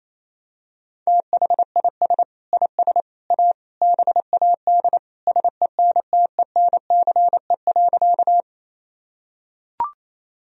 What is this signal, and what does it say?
Morse code, heard as a keyed tone, about 28 wpm, 700 Hz. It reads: THIS IS A BAD SENTENCE.
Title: Sentences from Top 1000 Words - Review Part 1 - 28wpm